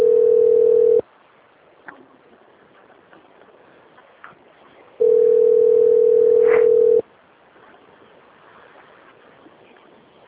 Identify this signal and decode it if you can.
Ringback tone (North America), 440 + 480 Hz.